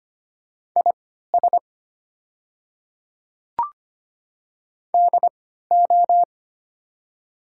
Morse code, heard as a keyed tone, about 25 wpm, spaced as for 14 wpm, 690 Hz. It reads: IS DO
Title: 3x Character Spacing - Top 200 Words - 25wpm